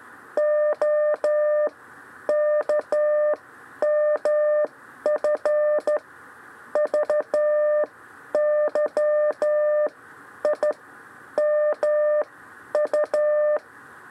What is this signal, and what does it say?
Morse code, heard as a keyed tone, about 13 wpm, spaced as for 9 wpm, 600 Hz. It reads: OKMFVYIMU
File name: 6 lezione cw 2